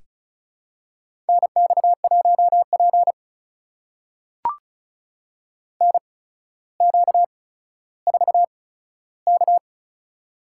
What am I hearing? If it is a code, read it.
Morse code, heard as a keyed tone, about 36 wpm, spaced as for 15 wpm, 700 Hz. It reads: NX1P NQ4K
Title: USA Call Signs - 4 Characters - 8x Character Spacing - 35wpm